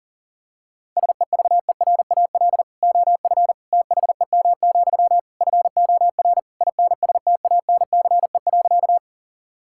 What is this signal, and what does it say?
Morse code, heard as a keyed tone, about 40 wpm, 700 Hz. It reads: SEVERAL OF THEM, FOR INSTANCE.
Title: Sentences from Top 1000 Words - Review Part 5 - 40wpm